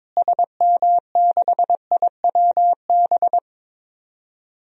Morse code, heard as a keyed tone, about 22 wpm, 700 Hz. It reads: SM6IWB
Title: DXCC-100 Call Signs - 6 Characters - Repeated 2x - 22wpm